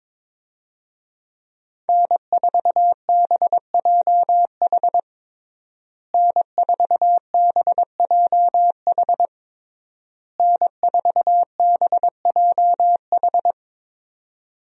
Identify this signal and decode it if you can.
Morse code, heard as a keyed tone, about 22 wpm, 700 Hz. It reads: N4BJH N4BJH N4BJH
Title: USA Call Signs - 5 Characters - Repeated 3x - 22wpm